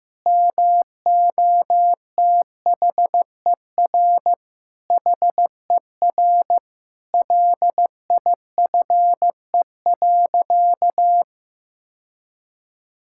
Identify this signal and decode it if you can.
Morse code, heard as a keyed tone, about 15 wpm, 700 Hz. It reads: MOTHER HER LIFE.
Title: Sentences from Top 1000 Words - Review Part 5 - 15wpm